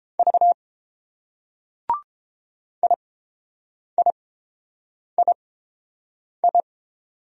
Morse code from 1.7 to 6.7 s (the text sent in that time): IIII